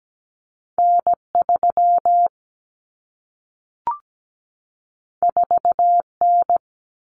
Morse code, heard as a keyed tone, about 17 wpm, 700 Hz. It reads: N3 4N
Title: Common Cut Numbers - 2-Digit 17wpm